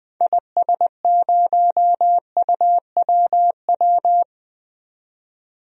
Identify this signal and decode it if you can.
Morse code, heard as a keyed tone, about 21 wpm, 700 Hz. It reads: IS0UWW
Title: DXCC-100 Call Signs - 6 Characters - 20wpm